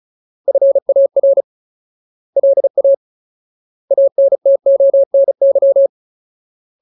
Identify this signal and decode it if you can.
Morse code, heard as a keyed tone, about 34 wpm, 550 Hz. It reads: FAR LA ANTONY
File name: Ultimate Word List / 35WPM Ultimate